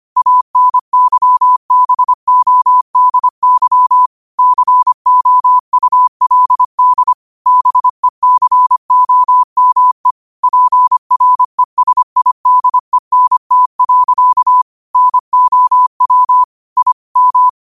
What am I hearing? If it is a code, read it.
Morse code, heard as a keyed tone, about 25 wpm, 1000 Hz. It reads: ANYBODY COULD BECOME PRESIDENT. NOW I M
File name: qotdfiles / 25wpm